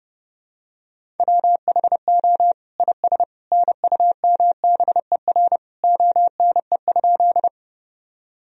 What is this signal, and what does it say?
Morse code, heard as a keyed tone, about 32 wpm, 700 Hz. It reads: WHO IS NUMBER ONE?
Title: Sentences from Top 1000 Words - Review Part 1 - 30wpm